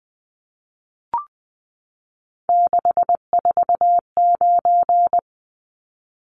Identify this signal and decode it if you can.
Morse code, heard as a keyed tone, about 20 wpm, 700 Hz. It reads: 649
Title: Numbers - 3-Digit 20wpm